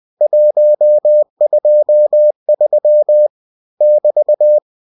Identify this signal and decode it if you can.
Morse code, heard as a keyed tone, about 20 wpm, 600 Hz. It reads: 123 <BT>
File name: cw-books / winnie-the-pooh / winnie-20wpm-page-0123